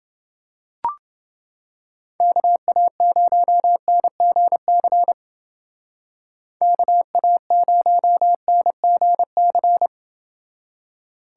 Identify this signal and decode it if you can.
Morse code, heard as a keyed tone, about 28 wpm, 700 Hz. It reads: KA0NGC KA0NGC